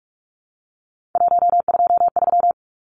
Morse code, heard as a keyed tone, about 44 wpm, 700 Hz. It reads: 123
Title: Numbers - 3-Digit 45wpm